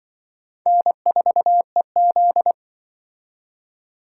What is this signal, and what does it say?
Morse code, heard as a keyed tone, about 27 wpm, 700 Hz. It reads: N4EZ